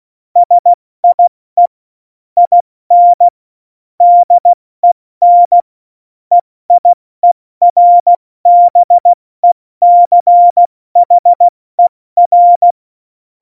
Morse code, heard as a keyed tone, about 16 wpm, 700 Hz. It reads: SIE IN DEN EIERBECHER
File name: Morsekurs / Witze / Witz 14. Mai 2019, 70 F80